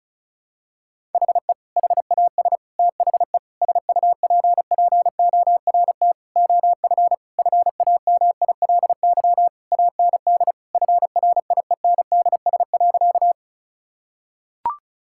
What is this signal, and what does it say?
Morse code, heard as a keyed tone, about 34 wpm, 700 Hz. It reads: HE HAS THE SUPPORT OF FAMILY AND FRIENDS.
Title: Sentences from Top 1000 Words - Review Part 5 - 35wpm